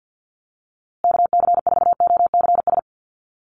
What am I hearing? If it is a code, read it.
Morse code, heard as a keyed tone, about 50 wpm, 700 Hz. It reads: KC4OCS
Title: USA Call Signs Repeated 2x 50wpm